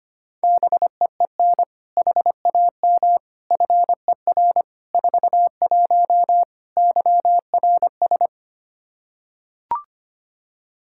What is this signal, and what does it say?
Morse code, heard as a keyed tone, about 25 wpm, 700 Hz. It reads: BEEN HAM FER 41 YRS
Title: Practice QSO - Exchange 3 - More Info - 25wpm